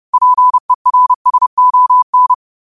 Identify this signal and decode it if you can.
Morse code, heard as a keyed tone, about 30 wpm, 1000 Hz. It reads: PERSON